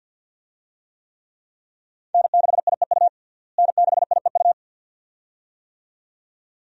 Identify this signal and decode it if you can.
Morse code, heard as a keyed tone, about 52 wpm, 700 Hz. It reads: N6IEU N6IEU
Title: USA Call Signs - 5 Characters - Repeated 2x - 50wpm